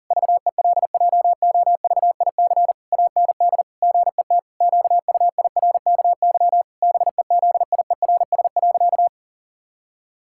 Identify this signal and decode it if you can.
Morse code, heard as a keyed tone, about 40 wpm, 700 Hz. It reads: VELJOVIC AND GET QUIRKY BEZIERS.
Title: Pangram Challenge v2 - 40wpm